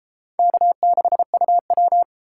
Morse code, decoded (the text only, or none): K6UW